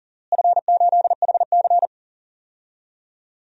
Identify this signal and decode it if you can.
Morse code, heard as a keyed tone, about 40 wpm, 700 Hz. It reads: F8HC
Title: DXCC-100 Call Signs - 4 Characters - Repeated 2x - 40wpm